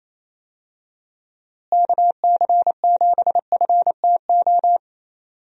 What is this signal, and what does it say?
Morse code, heard as a keyed tone, about 28 wpm, 700 Hz. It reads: KC7FTO
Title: USA Call Signs - 6 Characters - Repeated 2x - 28wpm